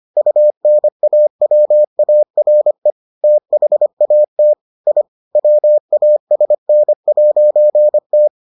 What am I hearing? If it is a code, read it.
Morse code, heard as a keyed tone, about 25 wpm, 600 Hz. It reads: UNAWARE THAT I WASN'T